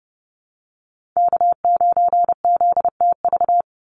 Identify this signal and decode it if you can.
Morse code, heard as a keyed tone, about 30 wpm, 700 Hz. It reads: K9ZTV
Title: Call Signs Repeated 2x - 30wpm